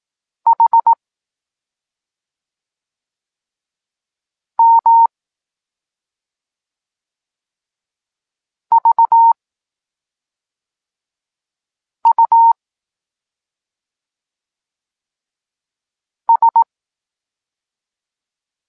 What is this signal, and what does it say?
Morse code, heard as a keyed tone, about 18 wpm, spaced as for 3 wpm, 920 Hz. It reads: HMVUS
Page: Tekst 8 – doskonalenie odbioru poznanych liter